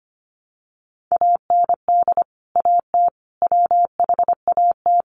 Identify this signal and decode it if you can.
Morse code, heard as a keyed tone, about 25 wpm, 700 Hz. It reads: AND AT WHAT
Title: Sentences from Top 1200 Words 25wpm